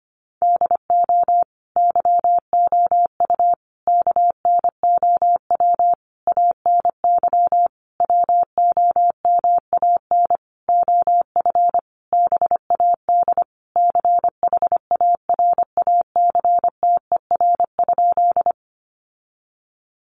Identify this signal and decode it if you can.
Morse code, heard as a keyed tone, about 25 wpm, 700 Hz. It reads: DO YOU KNOW ANY WOMAN OF BAD CHARACTER?
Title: Sentences from Top 1200 Words 25wpm